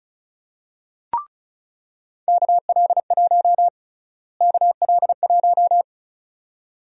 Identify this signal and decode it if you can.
Morse code, heard as a keyed tone, about 34 wpm, 700 Hz. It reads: KL1 KL1
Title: USA Call Sign Prefixes - Repeated 2x - 35wpm